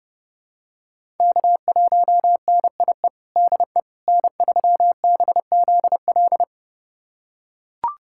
Morse code, heard as a keyed tone, about 30 wpm, 700 Hz. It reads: K1NIE DE N3BZL
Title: USA Callsign DE Callsign - 5 Characters - 30wpm